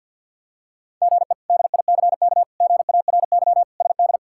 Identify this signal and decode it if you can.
Morse code, heard as a keyed tone, about 53 wpm, 700 Hz. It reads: GE DICK GARY SD
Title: K1USN SST QSO Practice - 50wpm